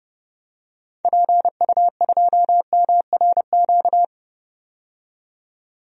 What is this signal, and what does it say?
Morse code, heard as a keyed tone, about 30 wpm, 700 Hz. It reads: PU2MRQ
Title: DXCC-100 Call Signs - 6 Characters - 30wpm